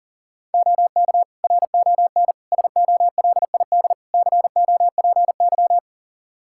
Morse code, heard as a keyed tone, about 40 wpm, 690 Hz. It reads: OK RON SOLID COPY
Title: Practice QSO - Exchange 3 - More Info - 40wpm